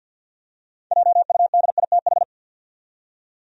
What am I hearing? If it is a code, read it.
Morse code, heard as a keyed tone, about 48 wpm, 700 Hz. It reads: JUDITH
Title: USA Common Names - 50wpm